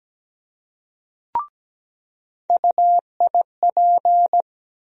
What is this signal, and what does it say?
Morse code, heard as a keyed tone, about 17 wpm, 700 Hz. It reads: UIP
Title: USA Call Sign Postfixes - 3 Letters 17wpm